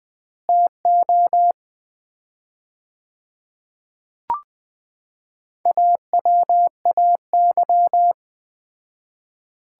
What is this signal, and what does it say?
Morse code, heard as a keyed tone, about 21 wpm, 700 Hz. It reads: TO AWAY
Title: 4x Word-Spacing - Sets of 3 Words - 20wpm